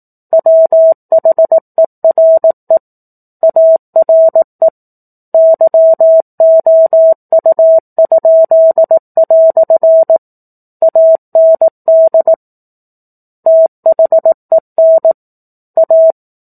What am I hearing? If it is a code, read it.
Morse code, heard as a keyed tone, about 18 wpm, 650 Hz. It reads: WHERE ARE YOU?" AND THEN A
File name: 04-Chapter-04 18 WPM